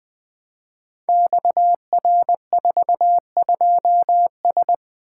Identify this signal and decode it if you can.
Morse code, heard as a keyed tone, about 20 wpm, 700 Hz. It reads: XR42S
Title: Code Groups - 5 Letter:Numbers - 20wpm